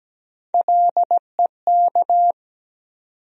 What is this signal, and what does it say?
Morse code, heard as a keyed tone, about 17 wpm, 700 Hz. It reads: LEK
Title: USA Call Sign Postfixes - 3 Letters 17wpm